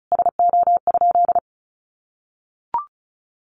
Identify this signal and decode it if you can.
Morse code, heard as a keyed tone, about 34 wpm, 700 Hz. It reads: SO?
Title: Sentences from Top 1200 Words 35wpm